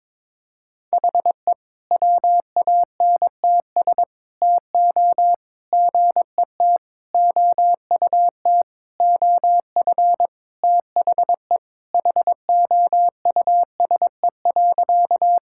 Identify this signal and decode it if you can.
Morse code, heard as a keyed tone, about 22 wpm, 700 Hz. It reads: HE WANTS TO GET OUT OF THE HOUSE.
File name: practice / 22wpm / Sentences with 4-Word Phrases - 22wpm